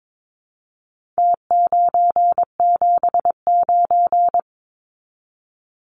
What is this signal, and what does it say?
Morse code, heard as a keyed tone, about 22 wpm, 700 Hz. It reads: T979
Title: Numbers - Contest Style - 4-Digit 22wpm